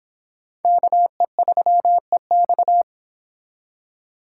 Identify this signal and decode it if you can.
Morse code, heard as a keyed tone, about 26 wpm, 700 Hz. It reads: KE3EX